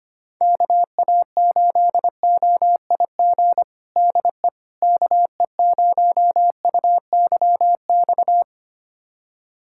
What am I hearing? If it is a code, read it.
Morse code, heard as a keyed tone, about 25 wpm, 700 Hz. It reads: KA8OIG DE KE0UYX
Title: USA Callsign DE Callsign - 6 Characters - 25wpm